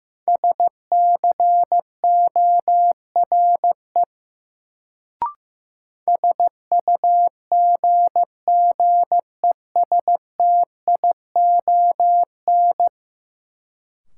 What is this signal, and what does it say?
Morse code, heard as a keyed tone, about 15 wpm, 700 Hz. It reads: SCORE SUGGESTION
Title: Sentences from Top 1600 Words - New Word Drill - 15wpm